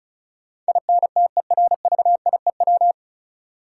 Morse code, heard as a keyed tone, about 36 wpm, 700 Hz. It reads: INTERVIEW
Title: Sentences from Top 1600 Words - New Word Drill Alt - 35wpm